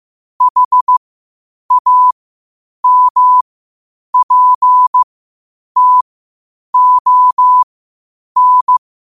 Morse code, heard as a keyed tone, about 15 wpm, spaced as for 8 wpm, 1000 Hz. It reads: HAMPTON
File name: qotdfiles / 8wpm